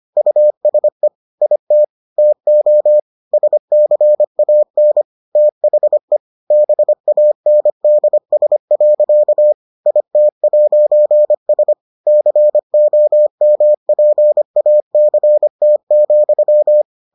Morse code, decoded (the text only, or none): USE IT TO SCAN THE BANDS. IT'S COMPACT,